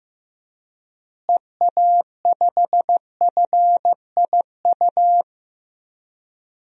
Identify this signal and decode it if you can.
Morse code, heard as a keyed tone, about 15 wpm, 700 Hz. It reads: EA5FIU
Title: DXCC-100 Call Signs - 6 Characters - Repeated 2x - 15wpm